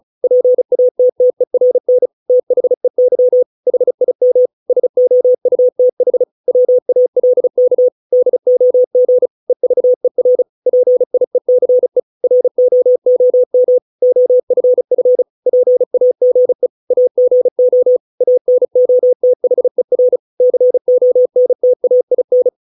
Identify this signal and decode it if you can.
Morse code, heard as a keyed tone, about 35 wpm, 500 Hz. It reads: PATTERN THEY HIM SOUTH WALK DOG EVER PIECE ROOM OFF PAGE AGO ANOTHER CONTAIN